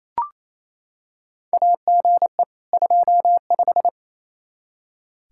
Morse code, heard as a keyed tone, about 28 wpm, 700 Hz. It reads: AGE 25
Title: Practice QSO - Exchange 3 - More Info - 28wpm